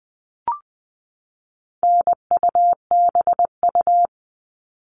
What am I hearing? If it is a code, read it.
Morse code, heard as a keyed tone, about 20 wpm, 700 Hz. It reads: NUBU
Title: All Cut Numbers - 4-Digit 20wpm